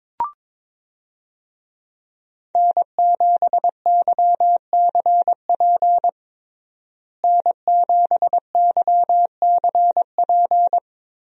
Morse code, heard as a keyed tone, about 22 wpm, 700 Hz. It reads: N7YCP N7YCP
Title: USA Call Signs - 5 Characters - Repeated 2x - 22wpm